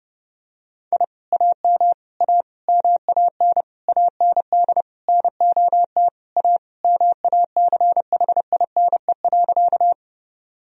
Morse code, heard as a keyed tone, about 30 wpm, 700 Hz. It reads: I AM A MAN AND NOT A MACHINE.
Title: Sentences from Top 1000 Words - Review Part 2 - 30wpm